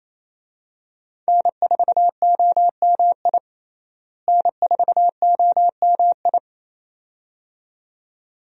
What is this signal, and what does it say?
Morse code, heard as a keyed tone, about 28 wpm, 700 Hz. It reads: N4OMI N4OMI
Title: USA Call Signs - 5 Characters - Repeated 2x - 28wpm